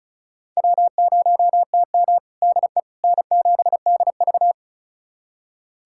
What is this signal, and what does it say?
Morse code, heard as a keyed tone, about 36 wpm, 700 Hz. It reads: W0TM DE N7DV
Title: USA Callsign DE Callsign - 4 Characters - 35wpm